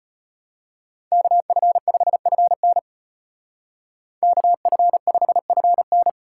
Morse code, heard as a keyed tone, about 31 wpm, 700 Hz. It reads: KF5FN KF5FN